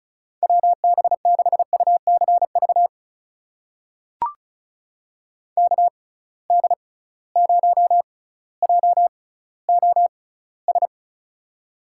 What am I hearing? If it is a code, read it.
Morse code, heard as a keyed tone, about 36 wpm, spaced as for 18 wpm, 700 Hz. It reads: WB6UCV KD0JOS
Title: USA Call Signs - 6 Characters - 6x Character Spacing - 35wpm